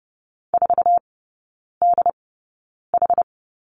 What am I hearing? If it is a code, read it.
Morse code, heard as a keyed tone, about 30 wpm, spaced as for 9 wpm, 700 Hz. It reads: 4DH
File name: 081 Lesson 14e - Number 2 - Three Characters 30wpm